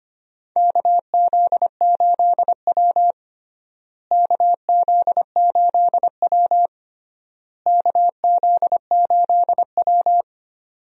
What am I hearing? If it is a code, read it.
Morse code, heard as a keyed tone, about 25 wpm, 700 Hz. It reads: KZ8W KZ8W KZ8W
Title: USA Call Signs - 4 Characters - Repeated 3x - 25wpm